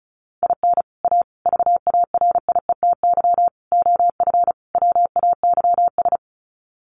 Morse code, heard as a keyed tone, about 34 wpm, 700 Hz. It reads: IN A VARIETY OF WAYS
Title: Sets of 5 Words Encore 35wpm